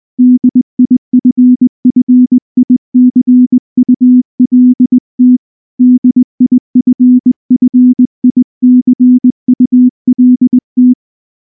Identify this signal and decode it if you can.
Morse code, heard as a keyed tone, about 19 wpm, 250 Hz. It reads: DIFFICULT DIFFICULT
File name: morse / morse-training-20WPM-4